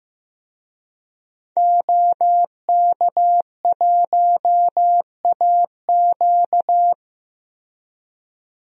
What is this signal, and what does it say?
Morse code, heard as a keyed tone, about 15 wpm, 700 Hz. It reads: OK1AQ
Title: DXCC-100 Call Signs - 5 Characters - Repeated 2x - 15wpm